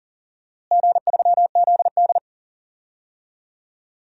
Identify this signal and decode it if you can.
Morse code, heard as a keyed tone, about 40 wpm, 700 Hz. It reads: G3ZD
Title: DXCC-100 Call Signs - 4 Characters - Repeated 3x - 40wpm